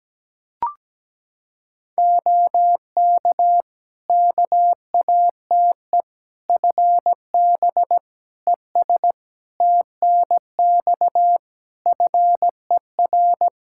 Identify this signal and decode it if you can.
Morse code, heard as a keyed tone, about 17 wpm, 700 Hz. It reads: OK KATE FB ES TNX FER
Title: Practice QSO - Exchange 2 - Info - 17wpm